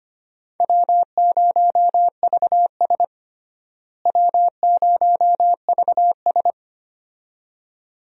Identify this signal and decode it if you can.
Morse code, heard as a keyed tone, about 25 wpm, 700 Hz. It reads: W0VS W0VS
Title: USA Call Signs - 4 Characters - Repeated 2x - 25wpm